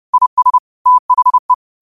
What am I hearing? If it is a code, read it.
Morse code, heard as a keyed tone, about 30 wpm, 1000 Hz. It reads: IS THE
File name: qotdfiles / 30wpm